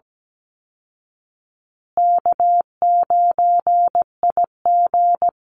Morse code, heard as a keyed tone, about 17 wpm, 700 Hz. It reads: K9IG
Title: Call Signs Repeated 2x - 17wpm